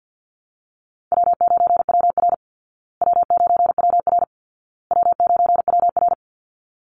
Morse code, heard as a keyed tone, about 48 wpm, 700 Hz. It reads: W9WR W9WR W9WR
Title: Call Signs Repeated 2x - 50wpm